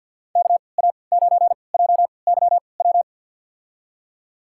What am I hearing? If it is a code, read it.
Morse code, heard as a keyed tone, about 50 wpm, spaced as for 28 wpm, 700 Hz. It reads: KA9JYW
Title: USA Call Signs - 6 Characters - 3x Character Spacing - 50wpm